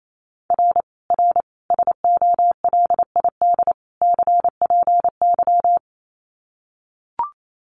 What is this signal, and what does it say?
Morse code, heard as a keyed tone, about 28 wpm, 700 Hz. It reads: R R SOLID CPY
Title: Common QSO Phrases 28wpm